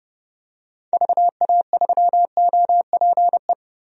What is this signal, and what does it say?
Morse code, heard as a keyed tone, about 30 wpm, 700 Hz. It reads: VA3OPE
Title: DXCC-100 Call Signs - 6 Characters - Repeated 2x - 30wpm